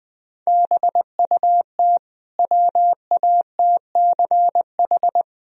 Morse code, heard as a keyed tone, about 20 wpm, 700 Hz. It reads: BUT WATCH
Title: Sentences from Top 1000 Words - Review Part 1 - 20wpm